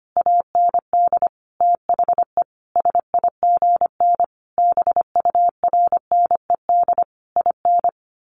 Morse code, heard as a keyed tone, about 25 wpm, 700 Hz. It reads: AND THE SIGN BURNED IN
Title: A Sound Of Thunder 25wpm